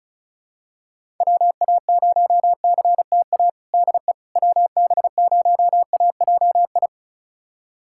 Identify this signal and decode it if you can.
Morse code, heard as a keyed tone, about 34 wpm, 700 Hz. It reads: WA0CTA DE WB0AJI
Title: USA Callsign DE Callsign - 6 Characters - 35wpm